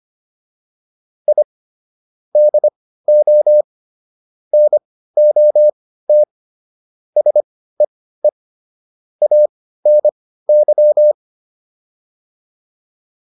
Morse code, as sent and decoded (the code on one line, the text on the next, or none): .. / -.. --- / -. --- - / ... . / .- -. -.--
I DO NOT SEE ANY